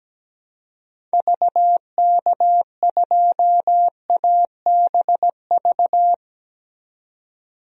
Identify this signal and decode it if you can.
Morse code, heard as a keyed tone, about 17 wpm, 700 Hz. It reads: VK2ABV
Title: DXCC-100 Call Signs - 6 Characters - 17wpm